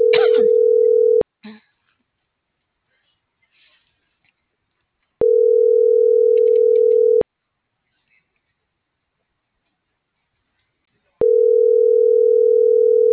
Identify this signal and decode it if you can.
Ringback tone (North America), 440 + 480 Hz.